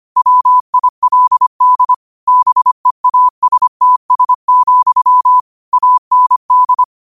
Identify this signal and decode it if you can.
Morse code, heard as a keyed tone, about 25 wpm, 1000 Hz. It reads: WILD BEASTS, AND
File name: qotdfiles / 25wpm